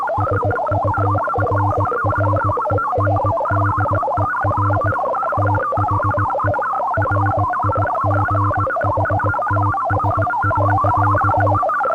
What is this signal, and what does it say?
Morse code, heard as a keyed tone, about 18 wpm, 310 Hz. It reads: SURRENDER THE RIGHTS.